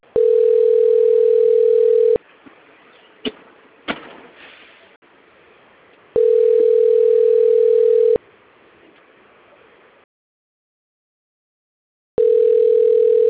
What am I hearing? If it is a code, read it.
Ringback tone (North America), 440 + 480 Hz.
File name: out- - - - - .16387